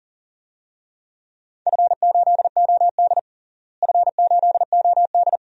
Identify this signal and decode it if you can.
Morse code, heard as a keyed tone, about 40 wpm, 690 Hz. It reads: F8OD F8OD
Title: DXCC-100 Call Signs - 4 Characters - Repeated 2x - 40wpm